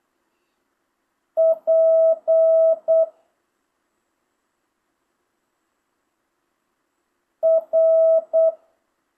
Morse code, heard as a keyed tone, about 8 wpm, 650 Hz. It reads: P R
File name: jh8cbh / 2025cw